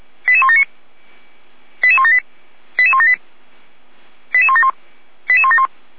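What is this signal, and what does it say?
Five-tone selective call (ZVEI), heard as quick runs of five tones, tones of about 70 ms. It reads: 88179 88178 88178 88171 88171
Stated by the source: ZVEI five-tone selective call received by SDR.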